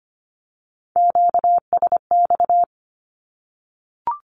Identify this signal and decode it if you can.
Morse code, heard as a keyed tone, about 25 wpm, 700 Hz. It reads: QSX
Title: Q-Codes Repeated 3x 25wpm